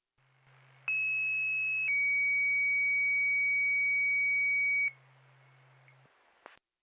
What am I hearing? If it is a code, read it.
Two-tone page, heard as a one-second tone, then a three-second tone, about 2600 then 2250 Hz. It is heard from 0.9 to 4.9 s.